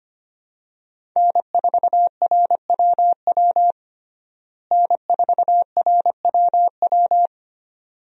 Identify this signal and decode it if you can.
Morse code, heard as a keyed tone, about 25 wpm, 700 Hz. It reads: N4RWW N4RWW
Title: USA Call Signs - 5 Characters - Repeated 2x - 25wpm